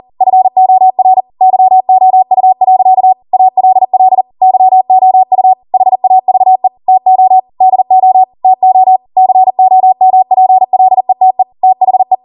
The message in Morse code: FOR YOU. ALL YOU HAVE TO DO TO COMPLETE THE